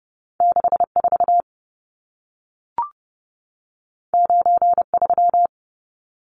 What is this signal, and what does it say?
Morse code, heard as a keyed tone, about 30 wpm, 700 Hz. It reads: 64 93
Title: Numbers - 2-Digit 30wpm